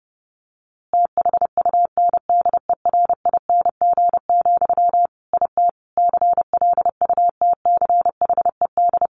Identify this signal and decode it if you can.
Morse code, heard as a keyed tone, about 30 wpm, 700 Hz. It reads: THUNDERING, IT CLUTCHED
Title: A Sound Of Thunder 30wpm